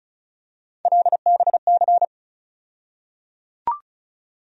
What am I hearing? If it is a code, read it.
Morse code, heard as a keyed tone, about 36 wpm, 700 Hz. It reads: LBC
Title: USA Call Sign Postfixes - 3 Letters 35wpm